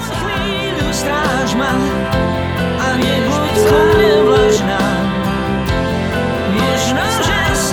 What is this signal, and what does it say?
Ringback tone (Europe, 425 Hz), with noise.